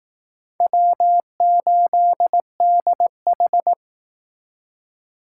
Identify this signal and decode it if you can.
Morse code, heard as a keyed tone, about 19 wpm, 700 Hz. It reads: W8DH